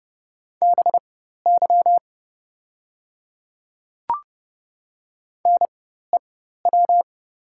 Morse code, heard as a keyed tone, about 31 wpm, spaced as for 14 wpm, 690 Hz. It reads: BY NEW
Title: 4x Character Spacing - Top 300 Words - 30wpm